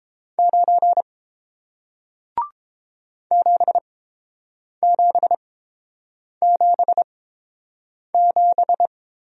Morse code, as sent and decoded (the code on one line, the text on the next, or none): ----. / --... --... --... --...
9 7777